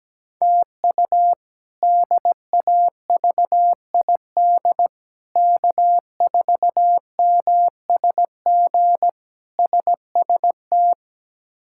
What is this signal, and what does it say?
Morse code, heard as a keyed tone, about 17 wpm, 700 Hz. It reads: TU DAVID K4MSG SST
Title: K1USN SST QSO Practice - 17wpm